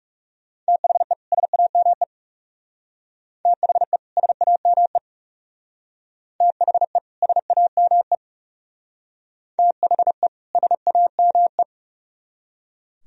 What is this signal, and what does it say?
Morse code, heard as a keyed tone, about 36 wpm, 700 Hz. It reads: THE SAME THE SAME THE SAME THE SAME
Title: Speed-Racing - Sets of 2 Words 30wpm